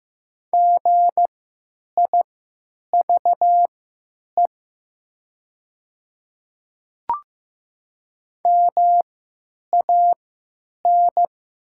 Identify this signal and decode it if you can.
Morse code, heard as a keyed tone, about 15 wpm, spaced as for 9 wpm, 700 Hz. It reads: GIVE MAN
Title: 3x Character Spacing - Top 200 Words - 15wpm